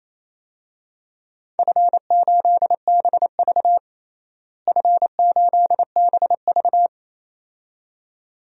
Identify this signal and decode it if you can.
Morse code, heard as a keyed tone, about 27 wpm, 700 Hz. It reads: F8BV F8BV
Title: DXCC-100 Call Signs - 4 Characters - Repeated 2x - 28wpm